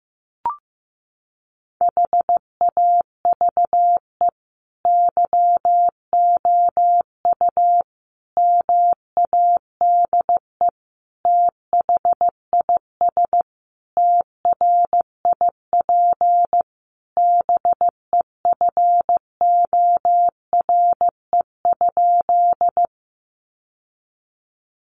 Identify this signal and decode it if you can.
Morse code, heard as a keyed tone, about 15 wpm, 700 Hz. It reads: HAVE YOU MADE THIS TRIP BEFORE?
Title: Sentences from Top 1300 Words 15wpm